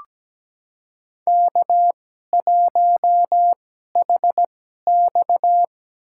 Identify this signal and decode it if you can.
Morse code, heard as a keyed tone, about 17 wpm, spaced as for 12 wpm, 700 Hz. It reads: K1HX
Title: USA Call Signs - 4 Characters - 2x Character Spacing - 17wpm